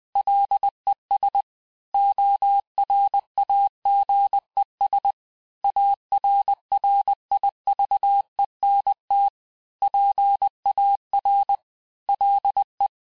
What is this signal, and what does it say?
Morse code, heard as a keyed tone, about 20 wpm, 800 Hz. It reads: LES ORAGES ARRIVENT PAR LE
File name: cours-du-mercredi / audios / cours-870-FIN-de-QSO-vitesse-19-MPM